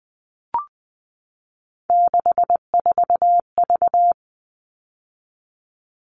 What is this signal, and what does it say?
Morse code, heard as a keyed tone, about 20 wpm, 700 Hz. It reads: 64V
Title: All Cut Numbers - 3-Digit 20wpm